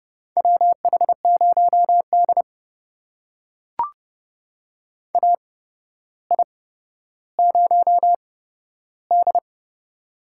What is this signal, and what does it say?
Morse code, heard as a keyed tone, about 29 wpm, spaced as for 13 wpm, 700 Hz. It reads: WH0D AI0D
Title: USA Call Signs - 4 Characters - 8x Character Spacing - 30wpm